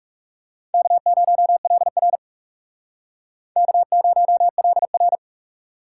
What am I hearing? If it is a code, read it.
Morse code, heard as a keyed tone, about 40 wpm, 700 Hz. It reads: K0LR K0LR